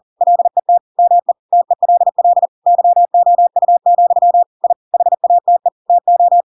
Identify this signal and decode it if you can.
Morse code, heard as a keyed tone, about 40 wpm, 700 Hz. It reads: LET ME TELL YOU, I HATE TO
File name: Pinocchio / Pinocchio ch4 40 40 ik1hge